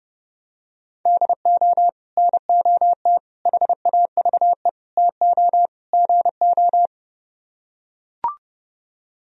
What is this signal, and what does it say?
Morse code, heard as a keyed tone, about 32 wpm, 700 Hz. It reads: DO NOT HAVE TO GO